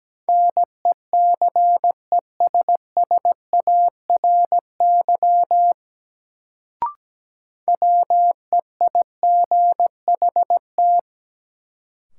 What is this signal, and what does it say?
Morse code, heard as a keyed tone, about 17 wpm, 700 Hz. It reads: NECESSARY WEIGHT
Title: Sentences from Top 800 Words - New Word Drill - 17wpm